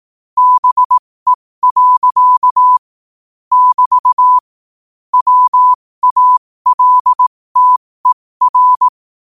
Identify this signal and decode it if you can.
Morse code, heard as a keyed tone, about 18 wpm, spaced as for 15 wpm, 1000 Hz. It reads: BE. <BT> WALTER
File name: qotdfiles / 15wpm